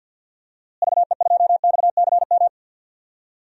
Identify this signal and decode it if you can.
Morse code, heard as a keyed tone, about 53 wpm, 700 Hz. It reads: VE2XCM
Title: DXCC-100 Call Signs - 6 Characters - Repeated 2x - 50wpm